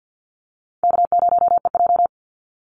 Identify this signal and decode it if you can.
Morse code, heard as a keyed tone, about 50 wpm, 700 Hz. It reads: K0EJ
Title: Easy Call Signs - 50wpm